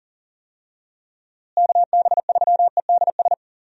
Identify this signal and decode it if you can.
Morse code, heard as a keyed tone, about 40 wpm, 700 Hz. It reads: KB3EDS
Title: USA Call Signs - 6 Characters - Repeated 3x - 40wpm